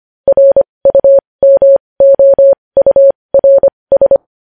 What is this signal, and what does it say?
Morse code, heard as a keyed tone, about 25 wpm, spaced as for 20 wpm, 550 Hz. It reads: RUMOURS
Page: News Headlines in Morse Code at 20 WPM